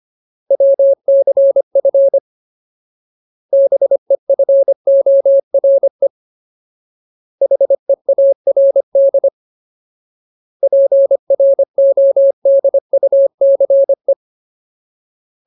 Morse code, heard as a keyed tone, about 25 wpm, 550 Hz. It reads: WCF BEFORE HEARD PRODUCE
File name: Ultimate Word List / 25WPM Ultimate